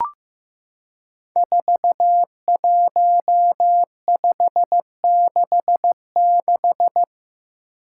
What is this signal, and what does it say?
Morse code, heard as a keyed tone, about 15 wpm, 700 Hz. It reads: 41566
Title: Five Challenging Letter-Number Combinations 15wpm